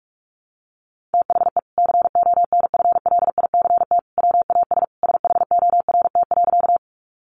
Morse code, heard as a keyed tone, about 44 wpm, 700 Hz. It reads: THE CONFLICT WAS SHORT.
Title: Sentences from Top 1300 Words 45wpm